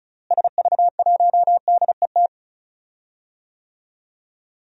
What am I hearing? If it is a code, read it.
Morse code, heard as a keyed tone, about 36 wpm, 700 Hz. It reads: SV1DET
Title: DXCC-100 Call Signs - 6 Characters - Repeated 3x - 35wpm